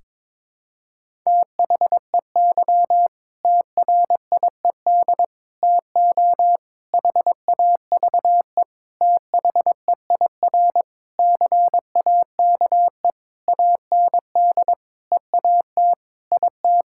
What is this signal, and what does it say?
Morse code, heard as a keyed tone, about 22 wpm, 700 Hz. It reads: THEY TRIED TO HAVE THEIR CAKE AND EAT IT
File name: practice / 22wpm / Sentences with 8-Word Phrases - 22wpm